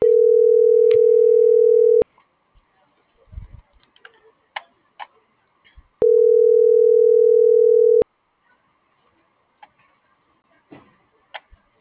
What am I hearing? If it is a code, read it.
Ringback tone (North America), 440 + 480 Hz.